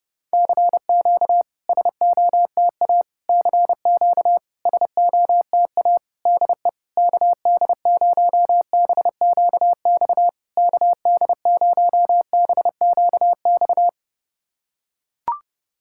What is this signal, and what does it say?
Morse code, heard as a keyed tone, about 30 wpm, 700 Hz. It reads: CQ SOTA CQ SOTA DE KD0BQX KD0BQX
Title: SOTA QSO Practice 30wpm